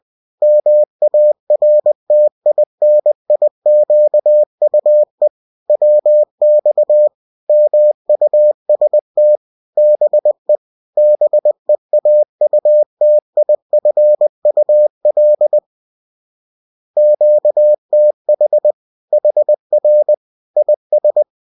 Morse code, heard as a keyed tone, about 20 wpm, 600 Hz. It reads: MARTINIQUE WX MUST BE BEAUTIFUL QTH HR IS